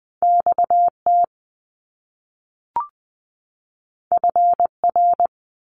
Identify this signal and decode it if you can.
Morse code, heard as a keyed tone, about 20 wpm, 700 Hz. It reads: XT FR
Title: Top Two Letter Combinations in Words 20wpm